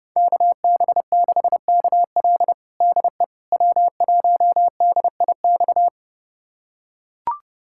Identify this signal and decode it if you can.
Morse code, heard as a keyed tone, about 32 wpm, 700 Hz. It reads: KB6KL DE W1DIX
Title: USA Callsign DE Callsign 30wpm